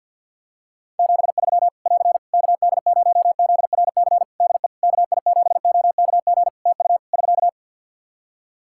Morse code, heard as a keyed tone, about 50 wpm, 700 Hz. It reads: 73 <AR> KD0ZRC DE KI7OKG TU <SK>
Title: K1USN SST QSO Practice 50wpm